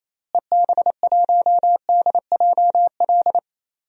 Morse code, heard as a keyed tone, about 27 wpm, 700 Hz. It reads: EB1DJL
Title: DXCC-100 Call Signs - 6 Characters - Repeated 2x - 28wpm